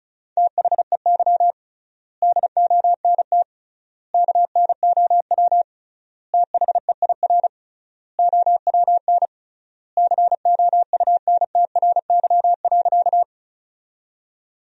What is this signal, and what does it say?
Morse code, heard as a keyed tone, about 36 wpm, 700 Hz. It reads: THEY DONT KNOW THEIR OWN COUNTRY.
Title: 3x Word-Spacing - Sentences from Top 300 Words - 35wpm